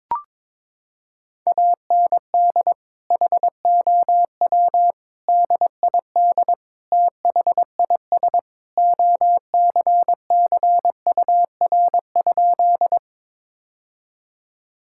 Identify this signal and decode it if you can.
Morse code, heard as a keyed tone, about 22 wpm, 700 Hz. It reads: AND HOW DID THIS OCCUR?
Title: Sentences from Top 1000 Words - Review Part 6 - 22wpm